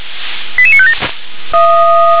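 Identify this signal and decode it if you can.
Five-tone selective call (ZVEI), heard as quick runs of five tones, tones of about 70 ms. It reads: 88957
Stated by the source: ZVEI five-tone selective call received with an SDR.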